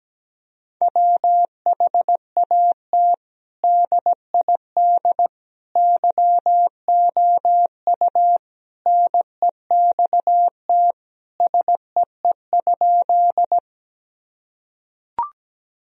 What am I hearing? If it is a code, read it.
Morse code, heard as a keyed tone, about 17 wpm, 700 Hz. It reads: WHAT DID YOU NEXT SEE?
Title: Sentences from Top 1000 Words - Review Part 1 - 17wpm